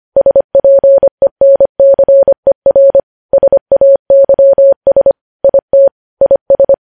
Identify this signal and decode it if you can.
Morse code, heard as a keyed tone, about 25 wpm, 550 Hz. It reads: SPENCER SAYS IT IS
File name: podcast / News headlines-1x-25x25